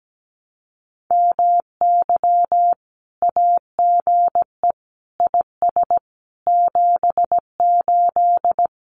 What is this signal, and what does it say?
Morse code, heard as a keyed tone, about 17 wpm, 700 Hz. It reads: MY AGE IS 78
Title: Common QSO Phrases Encore 17wpm